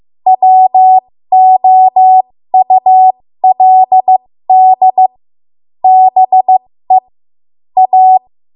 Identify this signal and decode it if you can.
Morse code, heard as a keyed tone, about 15 wpm, spaced as for 13 wpm, 750 Hz. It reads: WOULD BE A
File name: Archive / 13 WPM / 13WPM